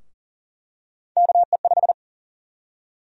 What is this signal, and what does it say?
Morse code, heard as a keyed tone, about 40 wpm, 700 Hz. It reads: KE5
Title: USA Call Sign Prefixes - Repeated 2x - 40wpm